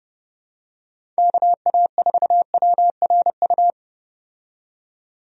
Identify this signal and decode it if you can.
Morse code, heard as a keyed tone, about 32 wpm, 700 Hz. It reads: KA4WRU
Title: USA Call Signs - 6 Characters - 30wpm